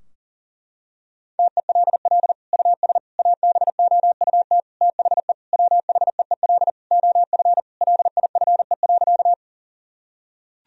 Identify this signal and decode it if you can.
Morse code, heard as a keyed tone, about 40 wpm, 700 Hz. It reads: TELL US ABOUT THE WHEEL OF LIFE.
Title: Sentences from Top 1000 Words - Review Part 3 - 40wpm